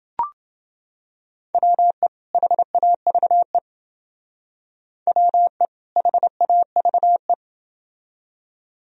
Morse code, heard as a keyed tone, about 30 wpm, 700 Hz. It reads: WE HAVE WE HAVE